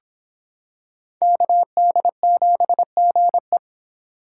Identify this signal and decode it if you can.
Morse code, heard as a keyed tone, about 25 wpm, 700 Hz. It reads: KD7GE